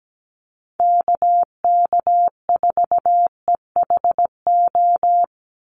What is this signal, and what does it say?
Morse code, heard as a keyed tone, about 17 wpm, 700 Hz. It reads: KK4EHO
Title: USA Call Signs 17wpm